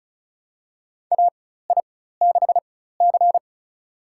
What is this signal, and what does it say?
Morse code, heard as a keyed tone, about 36 wpm, spaced as for 16 wpm, 700 Hz. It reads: AI6C
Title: USA Call Signs - 4 Characters - 4x Character Spacing - 35wpm